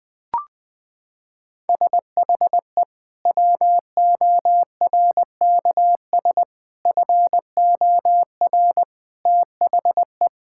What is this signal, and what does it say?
Morse code, heard as a keyed tone, about 21 wpm, 700 Hz. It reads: SHE WORKS FOR THE
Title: Sentences from Top 1000 Words - Review Part 5 - 20wpm